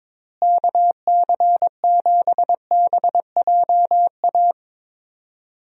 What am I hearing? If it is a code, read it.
Morse code, heard as a keyed tone, about 22 wpm, 700 Hz. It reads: KC7BJA
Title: USA Call Signs - 6 Characters - Repeated 3x - 22wpm